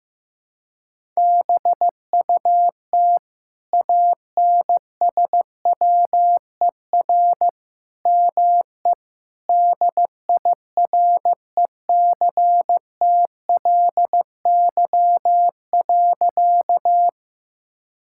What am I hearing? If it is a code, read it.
Morse code, heard as a keyed tone, about 15 wpm, 700 Hz. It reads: BUT ANSWER ME DIRECTLY.
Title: Sentences from Top 1000 Words - Review Part 5 - 15wpm